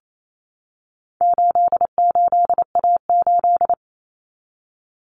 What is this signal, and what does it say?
Morse code, heard as a keyed tone, about 28 wpm, 700 Hz. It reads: 88A8
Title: Common Cut Numbers - 4-Digit 28wpm